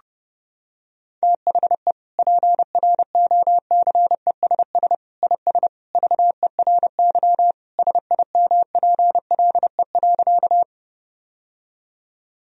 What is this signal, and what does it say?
Morse code, heard as a keyed tone, about 32 wpm, 700 Hz. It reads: THE PROCESS IS VERY SIMPLE.
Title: Sentences from Top 1000 Words - Review Part 3 - 30wpm